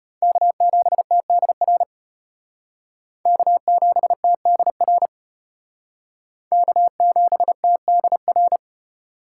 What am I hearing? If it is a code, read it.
Morse code, heard as a keyed tone, about 31 wpm, 700 Hz. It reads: K7TDR K7TDR K7TDR